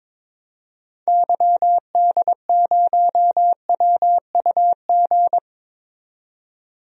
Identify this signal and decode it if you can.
Morse code, heard as a keyed tone, about 21 wpm, 700 Hz. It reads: YD0WUG
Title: DXCC-100 Call Signs - 6 Characters - 22wpm